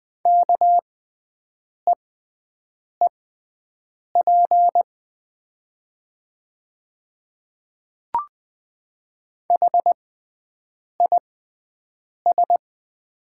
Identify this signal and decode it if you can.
Morse code, heard as a keyed tone, about 20 wpm, spaced as for 7 wpm, 700 Hz. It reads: KEEP HIS
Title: 6x Character Spacing - Top 300 Words - 20wpm